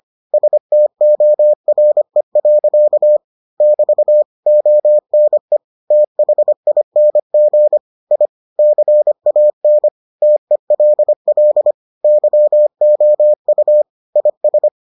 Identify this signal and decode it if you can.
Morse code, heard as a keyed tone, about 25 wpm, 600 Hz. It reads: STORE. <BT> ONE THING I CAN TELL YOU IS